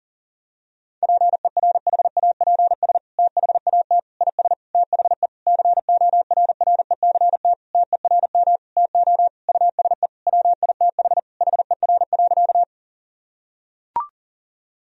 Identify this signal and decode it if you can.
Morse code, heard as a keyed tone, about 40 wpm, 700 Hz. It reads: PERHAPS THAT IS THE CORRECT TERM TO USE WITH HER.
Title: Sentences from Top 1000 Words - Review Part 4 - 40wpm